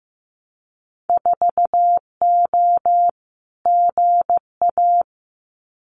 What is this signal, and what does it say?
Morse code, heard as a keyed tone, about 15 wpm, 700 Hz. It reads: 4O GA
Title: Winter Field Day Exchanges 15wpm